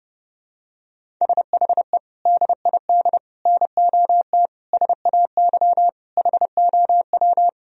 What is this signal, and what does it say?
Morse code, heard as a keyed tone, about 30 wpm, 700 Hz. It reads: SHE DID NOT SAY HOW
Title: Sentences from Top 1000 Words - Review Part 1 - 30wpm